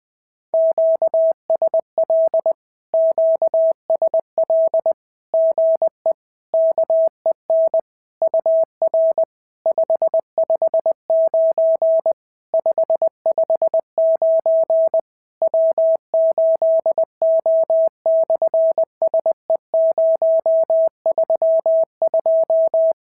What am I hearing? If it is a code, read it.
Morse code, heard as a keyed tone, about 19 wpm, 650 Hz. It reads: QSL QSL GE KEN UR 559 559 W8O/SE032
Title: SOTA QSO Practice 20wpm